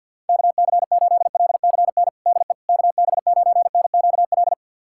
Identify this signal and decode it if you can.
Morse code, heard as a keyed tone, about 52 wpm, 700 Hz. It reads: KC8LCN DE KB9NQL
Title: USA Callsign DE Callsign - 6 Characters - 50wpm